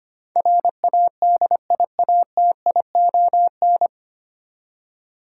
Morse code, heard as a keyed tone, about 25 wpm, 700 Hz. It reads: RADIATION
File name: practice / 25wpm / 9 Letter Words - First 3 Letters Given - 25wpm